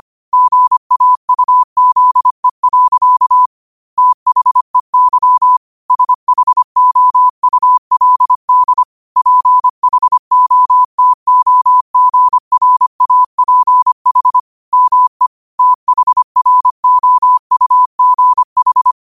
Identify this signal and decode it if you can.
Morse code, heard as a keyed tone, about 25 wpm, 1000 Hz. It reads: GAUZE. THEY SHOULD PHOTOGRAPH ME THROUGH